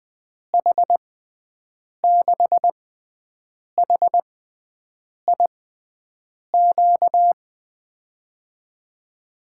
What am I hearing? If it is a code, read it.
Morse code, heard as a keyed tone, about 20 wpm, spaced as for 7 wpm, 700 Hz. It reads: H6HIQ
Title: USA Call Signs - 6 Characters - 6x Character Spacing - 20wpm